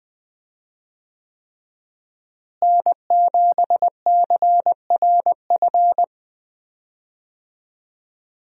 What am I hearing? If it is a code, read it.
Morse code, heard as a keyed tone, about 20 wpm, 700 Hz. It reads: N7CRF
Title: USA Call Signs - 5 Characters - Repeated 3x - 20wpm